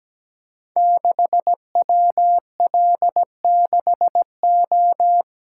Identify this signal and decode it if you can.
Morse code, heard as a keyed tone, about 17 wpm, 700 Hz. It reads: 6WL6O
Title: Code Groups - 5 Letter:Numbers - 17wpm